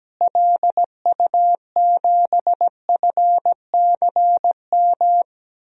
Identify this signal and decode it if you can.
Morse code, heard as a keyed tone, about 17 wpm, 700 Hz. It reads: LU7FCM
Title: DXCC-100 Call Signs - 6 Characters - Repeated 2x - 17wpm